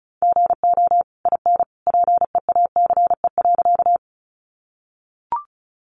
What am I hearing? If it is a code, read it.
Morse code, heard as a keyed tone, about 34 wpm, 700 Hz. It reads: GO IN PEACE.
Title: Sentences from Top 1500 Words 35wpm